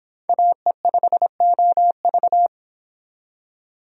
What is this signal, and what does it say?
Morse code, heard as a keyed tone, about 28 wpm, 700 Hz. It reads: AE5OV